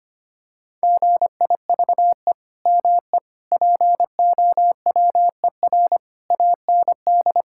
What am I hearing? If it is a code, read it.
Morse code, heard as a keyed tone, about 25 wpm, 700 Hz. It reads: GIVE ME POWER AND
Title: Sentences from Top 1000 Words - Review Part 5 - 25wpm